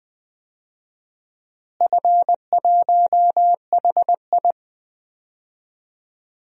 Morse code, heard as a keyed tone, about 20 wpm, 700 Hz. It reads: F1HI
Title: DXCC-100 Call Signs - 4 Characters - Repeated 3x - 20wpm